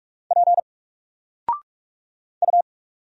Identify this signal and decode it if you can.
Morse code, heard as a keyed tone, about 44 wpm, 700 Hz. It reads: P U
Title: Single Character (Letter:Number:Punctuation:Prosign) 45wpm